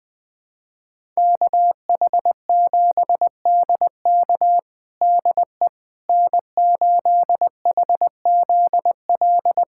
Morse code, heard as a keyed tone, about 20 wpm, 700 Hz. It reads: KH7DK DE N8HZL
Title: USA Callsign DE Callsign - 5 Characters - 20wpm